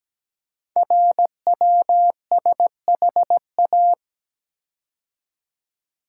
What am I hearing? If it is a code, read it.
Morse code, heard as a keyed tone, about 17 wpm, 700 Hz. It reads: RWSHA
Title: Code Groups - 5 Letters - 17wpm